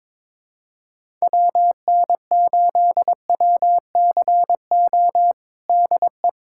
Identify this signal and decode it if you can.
Morse code, heard as a keyed tone, about 22 wpm, 700 Hz. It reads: WN8WCO DE
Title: USA Callsign DE Callsign - 6 Characters - 22wpm